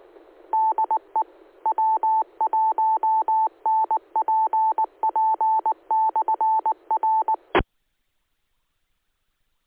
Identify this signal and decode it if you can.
Morse code, heard as a keyed tone, about 19 wpm, 880 Hz. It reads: DE W1NPP/R
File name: ares / AARES / aaresnet-feb-18.2016